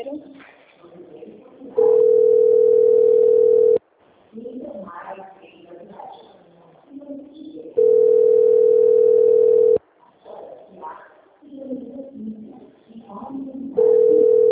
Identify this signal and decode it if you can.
Ringback tone (North America), 440 + 480 Hz, with speech.
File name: out- - - - - .55593